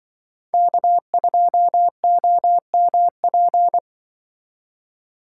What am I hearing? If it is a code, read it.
Morse code, heard as a keyed tone, about 23 wpm, 700 Hz. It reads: K2OMP
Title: Speed Racing - USA 1x3 Call Signs 20wpm